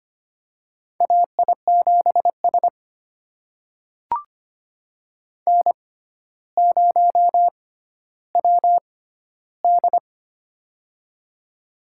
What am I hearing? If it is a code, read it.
Morse code, heard as a keyed tone, about 25 wpm, spaced as for 13 wpm, 700 Hz. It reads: AI7S N0WD
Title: USA Call Signs - 4 Characters - 6x Character Spacing - 25wpm